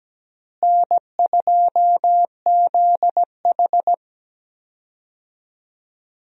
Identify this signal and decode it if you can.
Morse code, heard as a keyed tone, about 17 wpm, 700 Hz. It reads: N2ZH
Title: USA Call Signs - 4 Characters - Repeated 3x - 17wpm